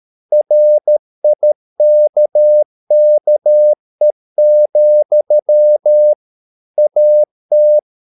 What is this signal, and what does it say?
Morse code, heard as a keyed tone, about 13 wpm, 600 Hz. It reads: RIKKE, AT
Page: Ferie 13 wpm